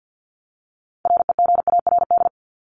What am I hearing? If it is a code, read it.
Morse code, heard as a keyed tone, about 50 wpm, 700 Hz. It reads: REGARD
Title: ED Suffix - Top 100 Words with Base Words 50wpm